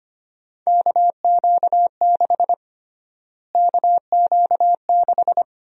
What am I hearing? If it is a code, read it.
Morse code, heard as a keyed tone, about 25 wpm, 700 Hz. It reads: KQ6 KQ6
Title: USA Call Sign Prefixes - Repeated 2x - 25wpm